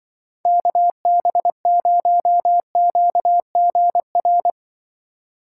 Morse code, heard as a keyed tone, about 24 wpm, 700 Hz. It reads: KB0QGR
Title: Speed Racing - USA Call Signs 20wpm